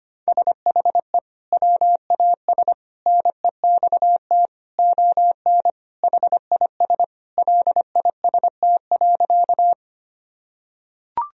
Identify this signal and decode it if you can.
Morse code, heard as a keyed tone, about 25 wpm, 700 Hz. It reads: SHE WAS NEXT ON HIS LIST.
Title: Sentences from Top 1000 Words - Review Part 2 - 25wpm